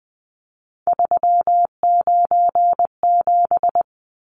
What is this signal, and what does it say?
Morse code, heard as a keyed tone, about 20 wpm, 700 Hz. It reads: 397
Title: Numbers - 3-Digit 20wpm